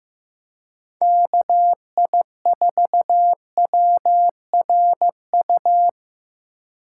Morse code, heard as a keyed tone, about 18 wpm, 700 Hz. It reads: KI4WRU
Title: Speed Racing - USA 2x3 Call Signs 15wpm